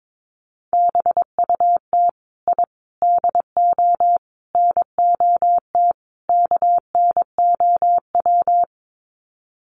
Morse code, heard as a keyed tone, about 22 wpm, 700 Hz. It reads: BUT I DO NOT KNOW
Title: Sets of 5 Words 22wpm